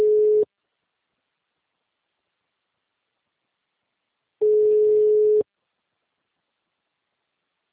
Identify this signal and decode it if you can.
Ringback tone (Europe, 425 Hz).